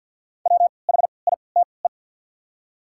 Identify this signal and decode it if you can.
Morse code, heard as a keyed tone, about 48 wpm, spaced as for 28 wpm, 700 Hz. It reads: WHITE
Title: 3x Character Spacing - Top 300 Words - 50wpm